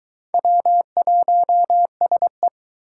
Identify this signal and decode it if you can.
Morse code, heard as a keyed tone, about 24 wpm, 700 Hz. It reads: W1SE